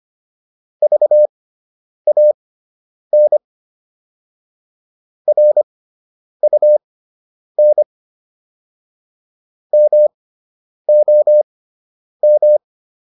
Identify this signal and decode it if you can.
Morse code, heard as a keyed tone, about 25 wpm, spaced as for 9 wpm, 600 Hz. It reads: VAN RUN MOM